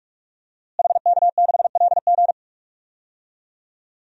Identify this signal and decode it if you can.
Morse code, heard as a keyed tone, about 46 wpm, 700 Hz. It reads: HK6LG
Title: DXCC-100 Call Signs - 5 Characters - Repeated 3x - 45wpm